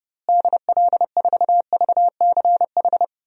Morse code, heard as a keyed tone, about 30 wpm, 700 Hz. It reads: DL4VCH
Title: DXCC-100 Call Signs - 6 Characters - Repeated 2x - 30wpm